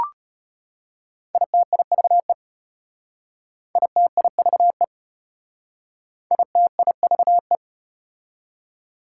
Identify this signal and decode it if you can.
Morse code, heard as a keyed tone, about 34 wpm, 700 Hz. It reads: ITIVE ITIVE ITIVE